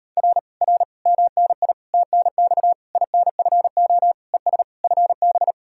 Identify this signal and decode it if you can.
Morse code, heard as a keyed tone, about 39 wpm, 700 Hz. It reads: R R MNI TNX INFO ES FB